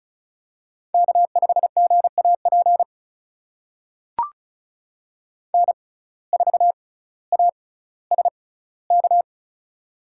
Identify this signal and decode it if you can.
Morse code, heard as a keyed tone, about 36 wpm, spaced as for 18 wpm, 700 Hz. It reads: K5GAP N4ASK
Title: USA Call Signs - 5 Characters - 6x Character Spacing - 35wpm